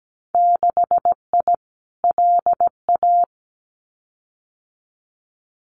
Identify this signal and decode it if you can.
Morse code, heard as a keyed tone, about 17 wpm, 700 Hz. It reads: 6I LA
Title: Winter Field Day Exchanges Encore 17wpm